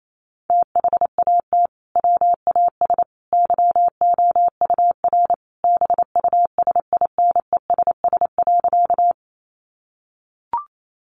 Morse code, heard as a keyed tone, about 28 wpm, 700 Hz. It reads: THAT WAS YOUR BUSINESS.
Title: Sentences from Top 600 Words 28wpm